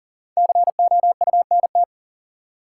Morse code, heard as a keyed tone, about 40 wpm, 700 Hz. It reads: COUNT
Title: Sentences from Top 1500 Words - New Word Drill Alt - 40wpm